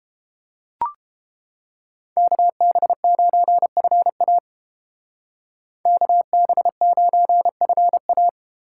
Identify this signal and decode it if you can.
Morse code, heard as a keyed tone, about 31 wpm, 700 Hz. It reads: KB9FA KB9FA